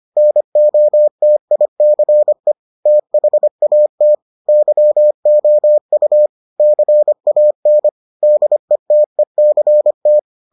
Morse code, heard as a keyed tone, about 25 wpm, 600 Hz. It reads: NOTICE THAT YOU CAN DETECT